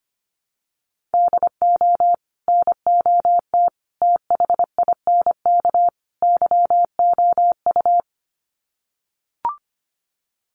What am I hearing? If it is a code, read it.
Morse code, heard as a keyed tone, about 25 wpm, 700 Hz. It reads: DO NOT THINK YOU
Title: Sets of 4 Words 25wpm